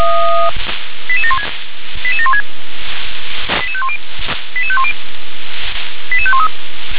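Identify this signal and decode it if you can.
Five-tone selective call (ZVEI), heard as quick runs of five tones, tones of about 70 ms. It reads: 88716 88716 88510 88510 88513
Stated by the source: ZVEI five-tone selective call received by SDR.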